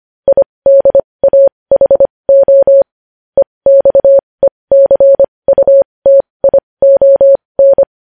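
Morse code, heard as a keyed tone, about 25 wpm, spaced as for 20 wpm, 550 Hz. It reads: IDAHO EXECUTION
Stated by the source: Morse code news headlines practice at 20 wpm.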